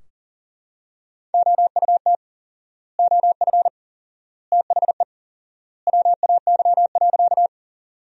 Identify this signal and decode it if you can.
Morse code, heard as a keyed tone, about 40 wpm, 700 Hz. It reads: OUT OF THE WAY.
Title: 4x Word-Spacing - Sentences from Top 100 Words - 40wpm